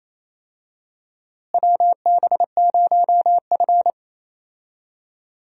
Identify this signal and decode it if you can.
Morse code, heard as a keyed tone, about 28 wpm, 700 Hz. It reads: WB0F
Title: USA Call Signs - 4 Characters - Repeated 2x - 28wpm